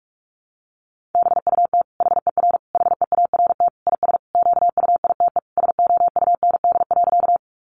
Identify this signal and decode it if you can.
Morse code, heard as a keyed tone, about 46 wpm, 700 Hz. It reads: BUT HER HEART IS QUITE SOUND.
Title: Sentences from Top 1200 Words 45wpm